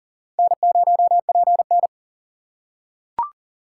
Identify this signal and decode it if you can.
Morse code, heard as a keyed tone, about 40 wpm, 700 Hz. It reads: N0PN
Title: USA Call Signs - 3x Character Spacing - 40wpm